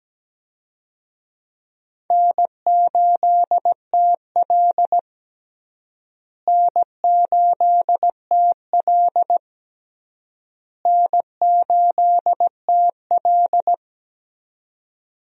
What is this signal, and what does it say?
Morse code, heard as a keyed tone, about 17 wpm, 700 Hz. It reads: N8TL N8TL N8TL
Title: USA Call Signs - 4 Characters - Repeated 3x - 17wpm